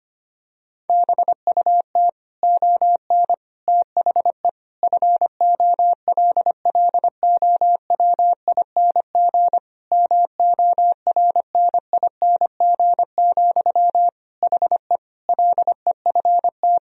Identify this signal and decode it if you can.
Morse code, heard as a keyed tone, about 25 wpm, 700 Hz. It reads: BUT ON THE FOLLOWING MORNING, HE LEFT
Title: Sentences from Top 1000 Words - Review Part 4 - 25wpm